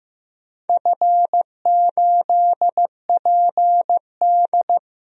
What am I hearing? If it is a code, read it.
Morse code, heard as a keyed tone, about 15 wpm, 700 Hz. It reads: F8PD
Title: DXCC-100 Call Signs - 4 Characters - 15wpm